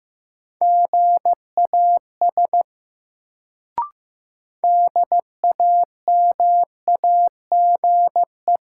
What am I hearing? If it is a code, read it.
Morse code, heard as a keyed tone, about 15 wpm, 700 Hz. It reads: GAS DAMAGE
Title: Sentences from Top 1100 Words - New Word Drill - 15wpm